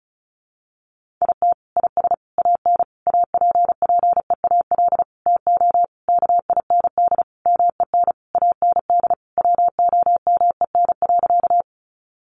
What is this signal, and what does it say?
Morse code, heard as a keyed tone, about 34 wpm, 700 Hz. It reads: IT IS AN APPEAL TO KIND MEN AND WOMEN.
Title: Sentences from Top 1300 Words 35wpm